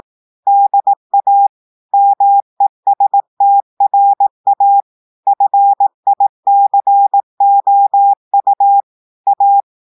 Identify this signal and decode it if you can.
Morse code, heard as a keyed tone, about 18 wpm, 800 Hz. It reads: DA MESTRA FICOU A
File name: audios / portugais / portugais texte en clair 1 vitesse 18 mots minutes